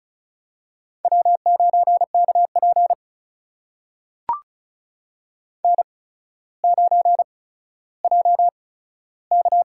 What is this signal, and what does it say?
Morse code, heard as a keyed tone, about 36 wpm, spaced as for 15 wpm, 700 Hz. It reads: W9KP N9JK
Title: USA Call Signs - 4 Characters - 8x Character Spacing - 35wpm